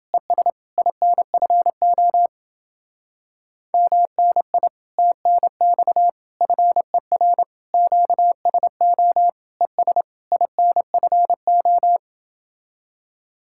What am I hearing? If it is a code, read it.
Morse code, heard as a keyed tone, about 30 wpm, 700 Hz. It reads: ES INFO MNI TNX FER QSO ES INFO